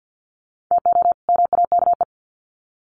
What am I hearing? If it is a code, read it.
Morse code, heard as a keyed tone, about 50 wpm, 700 Hz. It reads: TO MAKE
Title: Sets of 2 Words 50wpm